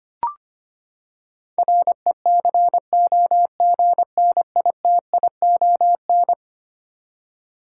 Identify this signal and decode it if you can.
Morse code, heard as a keyed tone, about 25 wpm, 700 Hz. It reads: RECOGNITION
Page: morse-code-ninja-practice-25wpm